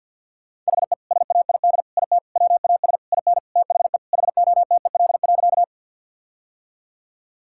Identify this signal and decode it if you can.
Morse code, heard as a keyed tone, about 50 wpm, 700 Hz. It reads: HE SAID IT WAS IN THE HOTEL.
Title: Sentences from Top 1600 Words 50wpm